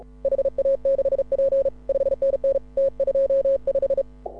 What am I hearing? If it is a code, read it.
Morse code, heard as a keyed tone, about 36 wpm, 530 Hz. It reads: HA6P 5NN T25